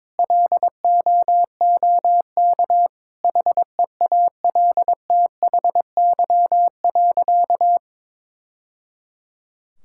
Morse code, heard as a keyed tone, about 22 wpm, 700 Hz. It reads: LOOK HEALTHY.
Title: Sentences from Top 1000 Words - Review Part 5 - 22wpm